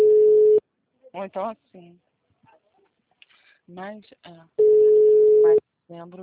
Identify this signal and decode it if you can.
Ringback tone (Europe, 425 Hz), with speech.